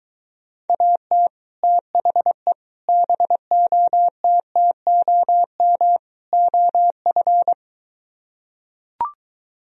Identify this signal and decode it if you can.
Morse code, heard as a keyed tone, about 24 wpm, 700 Hz. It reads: AT THE BOTTOM OF